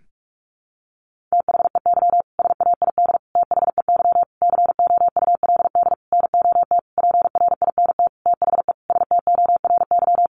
Morse code, heard as a keyed tone, about 44 wpm, 700 Hz. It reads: THEY SAID THEY COULD NOT PRINT THE STORY
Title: Sentences from Top 1400 Words 45wpm